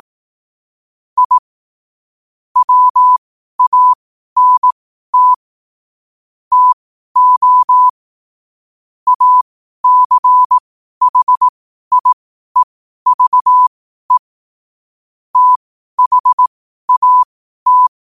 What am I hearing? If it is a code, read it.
Morse code, heard as a keyed tone, about 18 wpm, spaced as for 12 wpm, 1000 Hz. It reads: I WANT TO ACHIEVE THAT
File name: qotdfiles / 12wpm